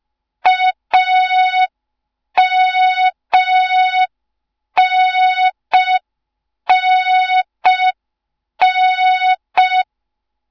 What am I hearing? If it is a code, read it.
Morse code, heard as a keyed tone, about 5 wpm, 3700 Hz. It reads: AMNNN
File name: Morze-HA2MN / hang / 07 GYAKORLAT